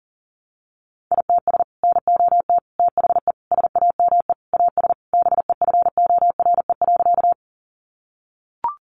Morse code, heard as a keyed tone, about 40 wpm, 700 Hz. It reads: ITS NOT THE SAME AS BEFORE.
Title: Sentences from Top 600 Words 40wpm